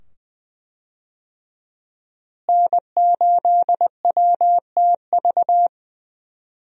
Morse code, heard as a keyed tone, about 20 wpm, 700 Hz. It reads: N8WTV
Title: USA Call Signs - 5 Characters - Repeated 2x - 20wpm